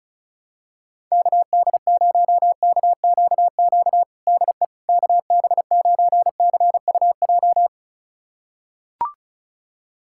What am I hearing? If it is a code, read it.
Morse code, heard as a keyed tone, about 36 wpm, 700 Hz. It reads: KD0KQQ DE KB9CUJ
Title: USA Callsign DE Callsign - 6 Characters - 35wpm